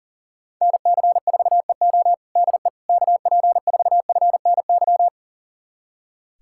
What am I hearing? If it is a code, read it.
Morse code, heard as a keyed tone, about 40 wpm, 700 Hz. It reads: NC4EO DE KP4FNY
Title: USA Callsign DE Callsign 40wpm